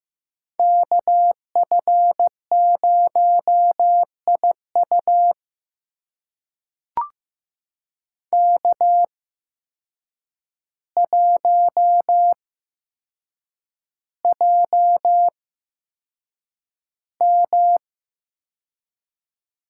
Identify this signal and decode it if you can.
Morse code, heard as a keyed tone, about 15 wpm, spaced as for 7 wpm, 700 Hz. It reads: KF0IU K1JM
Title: USA Call Signs - 8x Character Spacing - 15wpm